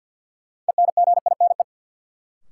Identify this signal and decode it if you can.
Morse code, heard as a keyed tone, about 52 wpm, 700 Hz. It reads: ENGINE
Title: Sentences from Top 1500 Words - New Word Drill - 50wpm